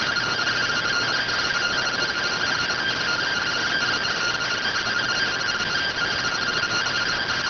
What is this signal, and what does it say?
Two-tone teleprinter signal, about 170 Hz shift.